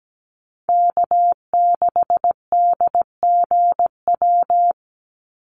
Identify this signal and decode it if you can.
Morse code, heard as a keyed tone, about 17 wpm, 700 Hz. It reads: K6DGW
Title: Call Signs Repeated 3x - 17wpm